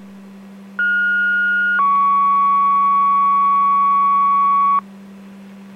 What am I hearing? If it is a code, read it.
Two-tone page, heard as a one-second tone, then a three-second tone, about 1430 then 1120 Hz. It is heard from 0.8 to 4.8 s.